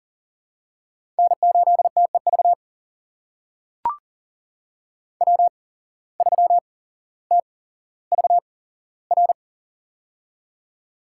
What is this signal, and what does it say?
Morse code, heard as a keyed tone, about 40 wpm, spaced as for 17 wpm, 700 Hz. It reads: N8TEV W3TVR
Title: USA Call Signs - 5 Characters - 8x Character Spacing - 40wpm